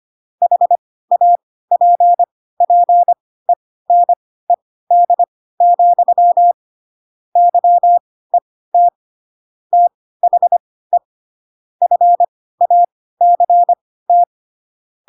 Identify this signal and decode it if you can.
Morse code, heard as a keyed tone, about 25 wpm, spaced as for 16 wpm, 700 Hz. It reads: HAPPENED, YET THE FACT
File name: Pinocchio / Pinocchio ch1 10 25 ik1hge